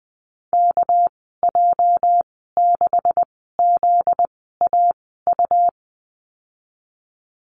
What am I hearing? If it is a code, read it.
Morse code, heard as a keyed tone, about 20 wpm, spaced as for 14 wpm, 700 Hz. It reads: KJ6ZAU
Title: 245 Lesson 39g - Letter z - Call Signs 20wpm